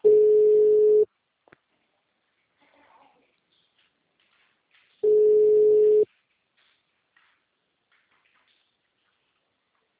Ringback tone (Europe, 425 Hz).
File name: out- -2002- - - .44155